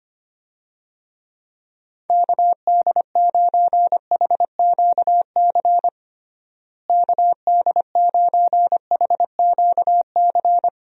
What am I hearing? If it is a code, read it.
Morse code, heard as a keyed tone, about 25 wpm, 700 Hz. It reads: KD9HQC KD9HQC
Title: USA Call Signs - 6 Characters - Repeated 3x - 25wpm